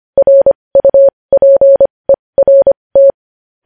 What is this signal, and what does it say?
Morse code, heard as a keyed tone, about 25 wpm, spaced as for 20 wpm, 550 Hz. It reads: RUPERT
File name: podcast / News headlines-1x-25x20